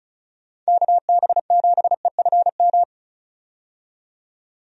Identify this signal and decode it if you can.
Morse code, heard as a keyed tone, about 34 wpm, 700 Hz. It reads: KB7EFM
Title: USA Call Signs - 6 Characters - 35wpm